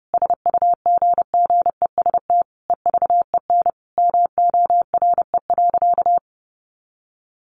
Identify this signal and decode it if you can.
Morse code, heard as a keyed tone, about 30 wpm, 700 Hz. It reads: SUGGEST EVEN MORE.
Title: Sentences from Top 1300 Words 30wpm